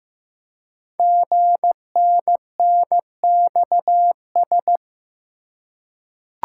Morse code, heard as a keyed tone, about 15 wpm, 700 Hz. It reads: GNNXS
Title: Code Groups - 5 Letters - 15wpm